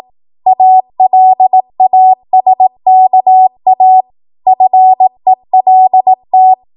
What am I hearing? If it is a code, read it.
Morse code, heard as a keyed tone, about 18 wpm, 750 Hz. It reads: ALASKA FELT
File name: Archive / 18 WPM / 18WPM